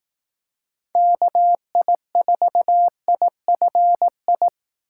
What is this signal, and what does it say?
Morse code, heard as a keyed tone, about 18 wpm, 700 Hz. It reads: KI4IFI
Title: Speed Racing - USA 2x3 Call Signs 15wpm